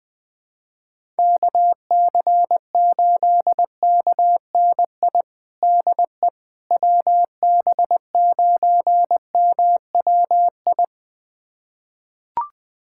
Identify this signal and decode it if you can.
Morse code, heard as a keyed tone, about 20 wpm, 700 Hz. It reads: KC8KNI DE WB9MWI
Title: USA Callsign DE Callsign 20wpm